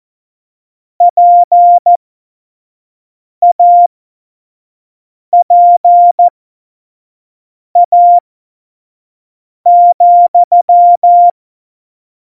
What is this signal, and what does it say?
Morse code, heard as a keyed tone, about 14 wpm, spaced as for 5 wpm, 700 Hz. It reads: PAPA,
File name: Morsekurs / Witze / Witz 22. August 17, Tempo 25 F70